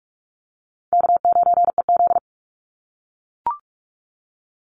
Morse code, heard as a keyed tone, about 44 wpm, 700 Hz. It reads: K9EZ
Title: NA SOTA Chaser Callsigns 45wpm